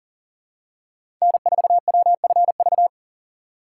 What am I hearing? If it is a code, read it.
Morse code, heard as a keyed tone, about 40 wpm, 700 Hz. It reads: N4WFV
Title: USA Call Signs - 5 Characters - 4x Character Spacing - 40wpm